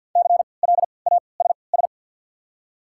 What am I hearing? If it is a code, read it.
Morse code, heard as a keyed tone, about 48 wpm, spaced as for 28 wpm, 700 Hz. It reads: CLASS
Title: 3x Character Spacing - Top 500 Words - 50wpm